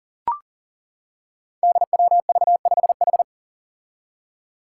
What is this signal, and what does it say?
Morse code, heard as a keyed tone, about 40 wpm, 690 Hz. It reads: DWV5H
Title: Five Challenging Letter-Number Combinations 40wpm